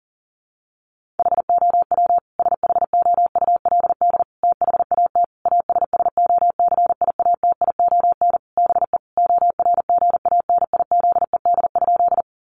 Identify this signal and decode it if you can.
Morse code, heard as a keyed tone, about 40 wpm, 700 Hz. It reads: HOW SHOULD THAT ASSOCIATION BE ORGANIZED?
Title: Sentences from Top 1300 Words 40wpm